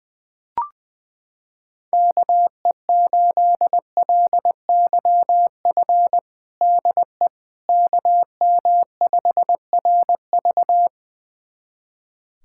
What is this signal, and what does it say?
Morse code, heard as a keyed tone, about 20 wpm, 700 Hz. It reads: KE8LYF DE KM5RV
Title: USA Callsign DE Callsign 20wpm